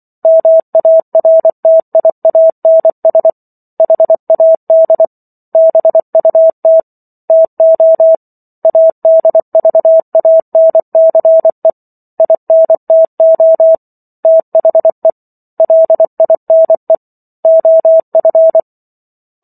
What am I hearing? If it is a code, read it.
Morse code, heard as a keyed tone, about 24 wpm, 650 Hz. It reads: MARTIANS HAD BUT TO ADVANCE INTO THE LINE OF